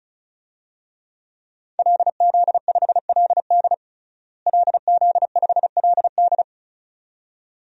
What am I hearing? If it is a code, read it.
Morse code, heard as a keyed tone, about 36 wpm, 700 Hz. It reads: LZ5LD LZ5LD
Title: DXCC-100 Call Signs - 5 Characters - Repeated 2x - 35wpm